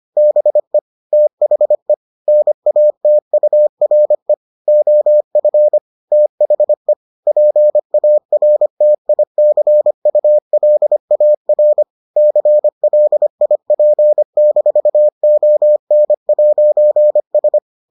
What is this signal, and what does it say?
Morse code, heard as a keyed tone, about 25 wpm, 600 Hz. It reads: BE THE NATURE OF THE PARTICULAR CLIP-ON'S